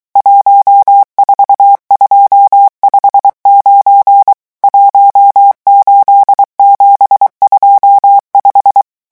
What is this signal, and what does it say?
Morse code, heard as a keyed tone, about 23 wpm, 800 Hz. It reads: 14259 18725
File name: morse mp3 / revision11 chiffres vitesse 22